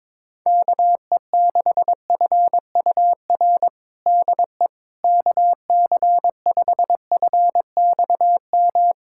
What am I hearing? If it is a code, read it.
Morse code, heard as a keyed tone, about 22 wpm, 700 Hz. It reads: KE6FUR DE KC5FXM
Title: USA Callsign DE Callsign 22wpm